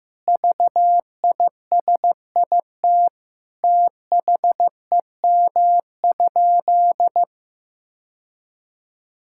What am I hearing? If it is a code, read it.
Morse code, heard as a keyed tone, about 15 wpm, 700 Hz. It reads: VISIT THEM?
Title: Sentences from Top 1000 Words - Review Part 6 - 15wpm